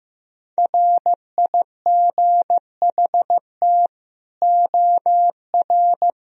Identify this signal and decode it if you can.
Morse code, heard as a keyed tone, about 16 wpm, 700 Hz. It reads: RIGHT OR
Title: Sentences from Top 1000 Words - Review Part 4 - 15wpm